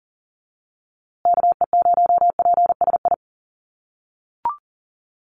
Morse code, heard as a keyed tone, about 40 wpm, 700 Hz. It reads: KE0PSI
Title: USA Call Signs Repeated 2x 40wpm